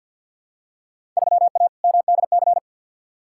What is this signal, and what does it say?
Morse code, heard as a keyed tone, about 47 wpm, 700 Hz. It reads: 3A MDC
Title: Field Day Exchanges - 50wpm